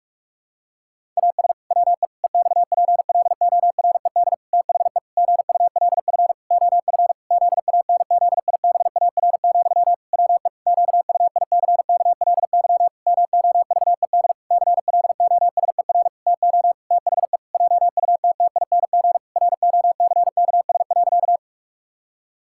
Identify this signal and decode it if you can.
Morse code, heard as a keyed tone, about 46 wpm, 700 Hz. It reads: AS WE EXPLORED THE GULF OF ZANZIBAR, WE QUICKLY MOVED CLOSER TO THE JUTTING ROCKS.
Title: Pangram Challenge v2 - 45wpm